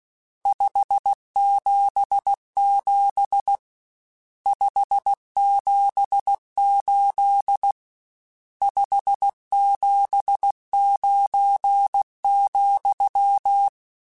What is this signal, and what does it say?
Morse code, heard as a keyed tone, about 16 wpm, 800 Hz. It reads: 577 578 579,
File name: RSTs / RST-mp3s / Various RST @15